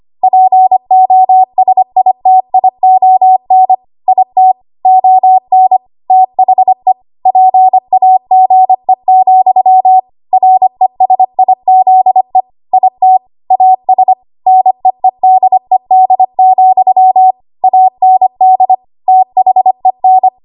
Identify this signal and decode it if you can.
Morse code, heard as a keyed tone, about 25 wpm, 750 Hz. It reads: POSITION IT ON THE PAGE, RESIZE IT AS NEEDED, AND THEN
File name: Archive / 25 WPM / 25WPM